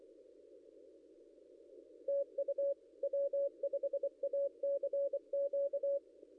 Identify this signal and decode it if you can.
Morse code, heard as a keyed tone, about 24 wpm, 560 Hz. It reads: TU W5ACQ
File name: contests / ss-cw-k5tr-op-n5rz -0800